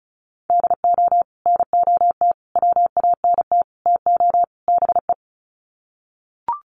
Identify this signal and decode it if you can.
Morse code, heard as a keyed tone, about 34 wpm, 700 Hz. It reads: DO NOT WANT TO BE
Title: Sets of 5 Words 35wpm